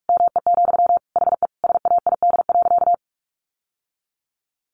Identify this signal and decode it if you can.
Morse code, heard as a keyed tone, about 44 wpm, 700 Hz. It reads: ME, HE SAID.
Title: Sentences from Top 600 Words 45wpm